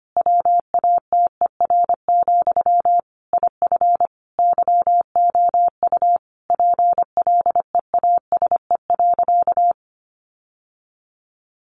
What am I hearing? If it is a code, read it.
Morse code, heard as a keyed tone, about 25 wpm, 700 Hz. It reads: WATER, IF YOU PLEASE.
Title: Sentences from Top 1500 Words 25wpm